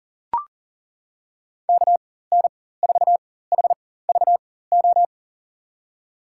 Morse code, heard as a keyed tone, about 41 wpm, spaced as for 19 wpm, 700 Hz. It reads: KN4HVO
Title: USA Call Signs - 6 Characters - 4x Character Spacing - 40wpm